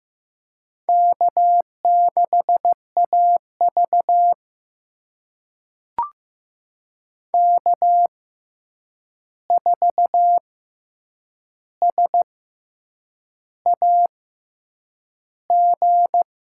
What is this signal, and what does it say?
Morse code, heard as a keyed tone, about 15 wpm, spaced as for 7 wpm, 700 Hz. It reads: K6AV K4SAG
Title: USA Call Signs - 6x Character Spacing - 15wpm